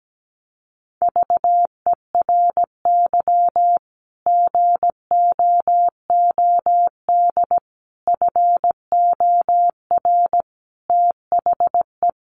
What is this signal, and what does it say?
Morse code, heard as a keyed tone, about 17 wpm, 700 Hz. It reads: VERY GOOD FOR THE
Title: Sentences from Top 1100 Words 17wpm